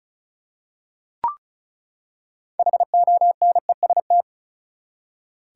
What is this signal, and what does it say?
Morse code, heard as a keyed tone, about 36 wpm, 700 Hz. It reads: HONEST
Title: Six Letter Words - 35wpm